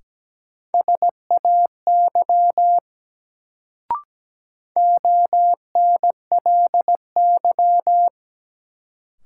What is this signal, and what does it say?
Morse code, heard as a keyed tone, about 17 wpm, 700 Hz. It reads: SAY ONLY
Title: Sentences from Top 200 Words - New Word Drill - 17wpm